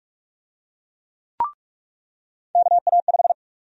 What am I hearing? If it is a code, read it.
Morse code, heard as a keyed tone, about 46 wpm, 700 Hz. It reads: KA5
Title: USA Call Sign Prefixes - 45wpm